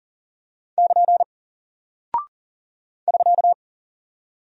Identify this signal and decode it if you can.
Morse code, heard as a keyed tone, about 40 wpm, 700 Hz. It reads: <KN> <SK>
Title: Single Character (Letter:Number:Punctuation:Prosign) Alt 40wpm